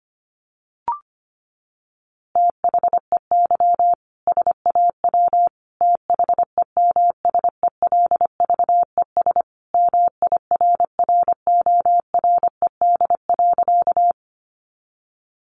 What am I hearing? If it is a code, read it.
Morse code, heard as a keyed tone, about 25 wpm, 700 Hz. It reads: THEY SAW THEMSELVES MIRRORED.
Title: A Sound Of Thunder 25wpm